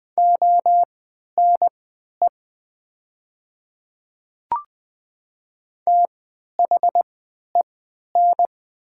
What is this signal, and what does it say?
Morse code, heard as a keyed tone, about 20 wpm, spaced as for 11 wpm, 700 Hz. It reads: ONE THEN
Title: 3x Character Spacing - Top 100 Words - 20wpm